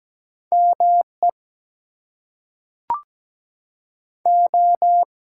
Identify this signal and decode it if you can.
Morse code, heard as a keyed tone, about 17 wpm, 700 Hz. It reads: ME O
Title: Two Letter Words - 17wpm